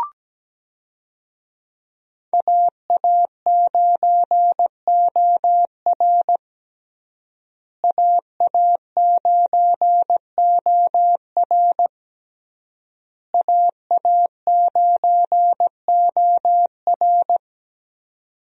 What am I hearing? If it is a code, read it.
Morse code, heard as a keyed tone, about 17 wpm, 700 Hz. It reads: AA9OR AA9OR AA9OR
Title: USA Call Signs - 5 Characters - Repeated 3x - 17wpm